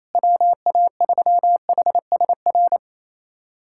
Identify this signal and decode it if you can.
Morse code, heard as a keyed tone, about 28 wpm, 700 Hz. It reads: WA3HSR
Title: USA Callsign DE Callsign 28wpm